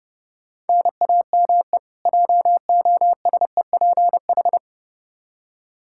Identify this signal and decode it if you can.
Morse code, heard as a keyed tone, about 32 wpm, 700 Hz. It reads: NAME JOSEPH
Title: Practice QSO - Exchange 1 - Beginning - 30wpm